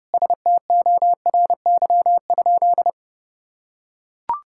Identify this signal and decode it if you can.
Morse code, heard as a keyed tone, about 30 wpm, 700 Hz. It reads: STORY?
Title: Sentences from Top 1000 Words - Review Part 5 - 30wpm